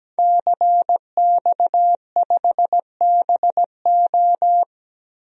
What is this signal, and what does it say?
Morse code, heard as a keyed tone, about 17 wpm, 700 Hz. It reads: CX5BO
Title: DXCC-100 Call Signs - 5 Characters - Repeated 3x - 17wpm